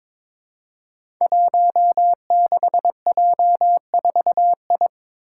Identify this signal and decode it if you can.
Morse code, heard as a keyed tone, about 22 wpm, 700 Hz. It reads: 16J4I
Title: Five Challenging Letter-Number Combinations 22wpm